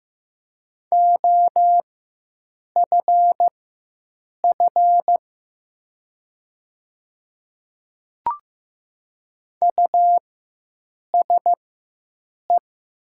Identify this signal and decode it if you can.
Morse code, heard as a keyed tone, about 15 wpm, spaced as for 7 wpm, 700 Hz. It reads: OFF USE
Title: 4x Character Spacing - Top 200 Words - 15wpm